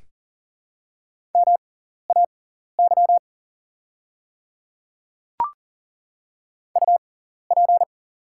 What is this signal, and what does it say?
Morse code, heard as a keyed tone, about 37 wpm, spaced as for 13 wpm, 700 Hz. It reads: MAY UP
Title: 6x Character Spacing - Top 100 Words - 40wpm